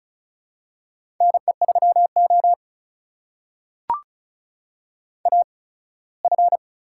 Morse code, heard as a keyed tone, about 36 wpm, spaced as for 21 wpm, 700 Hz. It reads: NE3O AF
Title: USA Call Signs - 8x Character Spacing - 35wpm